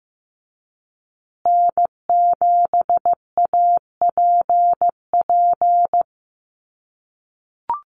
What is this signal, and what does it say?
Morse code, heard as a keyed tone, about 15 wpm, 700 Hz. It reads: N7APP
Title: USA Call Signs Repeated 2x 15wpm